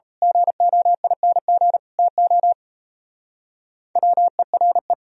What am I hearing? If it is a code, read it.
Morse code, heard as a keyed tone, about 32 wpm, 700 Hz. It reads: GOING TO WERE
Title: Speed-Racing - Sets of 3 Words 28wpm